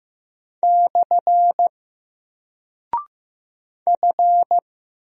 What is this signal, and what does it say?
Morse code, heard as a keyed tone, about 15 wpm, 700 Hz. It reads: / F
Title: Single Character (Letter:Number:Punctuation:Prosign) Alt 15wpm